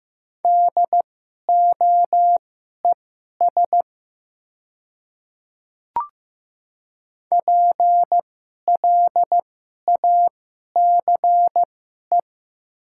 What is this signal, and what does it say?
Morse code, heard as a keyed tone, about 15 wpm, spaced as for 11 wpm, 700 Hz. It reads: DOES PLACE
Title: 2x Character Spacing - Top 300 Words - 15wpm